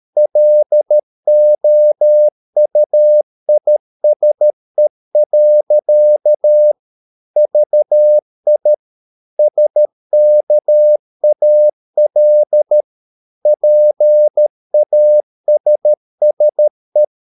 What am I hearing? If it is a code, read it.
Morse code, heard as a keyed tone, about 13 wpm, 600 Hz. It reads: LOUISE. VI SKAL PASSE
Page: Ferie 13 wpm